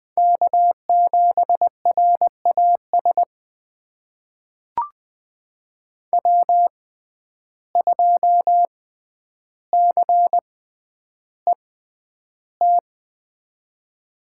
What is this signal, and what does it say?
Morse code, heard as a keyed tone, about 20 wpm, spaced as for 10 wpm, 700 Hz. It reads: K7RAS W2CET
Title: USA Call Signs - 6x Character Spacing - 20wpm